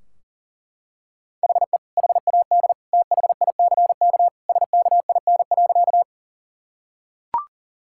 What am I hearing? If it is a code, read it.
Morse code, heard as a keyed tone, about 40 wpm, 700 Hz. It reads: HE HAD THICK SKIN.
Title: Sentences from Top 1600 Words 40wpm